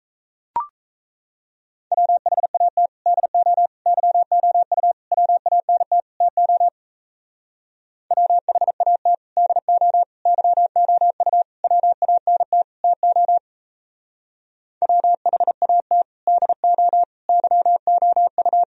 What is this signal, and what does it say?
Morse code, heard as a keyed tone, about 32 wpm, 700 Hz. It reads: WHAT DO YOU WANT TO WHAT DO YOU WANT TO WHAT DO YOU
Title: Speed-Racing - Sets of 5 Words 28wpm